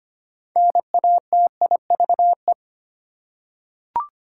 Morse code, heard as a keyed tone, about 25 wpm, 700 Hz. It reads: NATIVE
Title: Six Letter Words - 25wpm